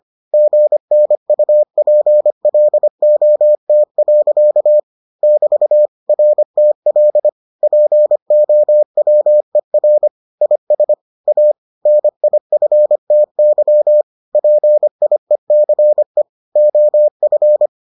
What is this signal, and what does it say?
Morse code, heard as a keyed tone, about 25 wpm, 600 Hz. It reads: GNUPLOT. <BT> RTL POWER IS A NIFTY PIECE OF